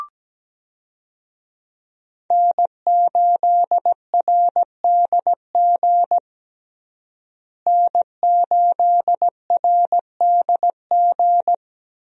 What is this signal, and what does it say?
Morse code, heard as a keyed tone, about 17 wpm, 700 Hz. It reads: N8RDG N8RDG
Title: USA Call Signs - 5 Characters - Repeated 2x - 17wpm